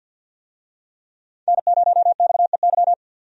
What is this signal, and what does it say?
Morse code, heard as a keyed tone, about 46 wpm, 700 Hz. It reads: N0XEY
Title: USA Call Signs - 5 Characters - 50wpm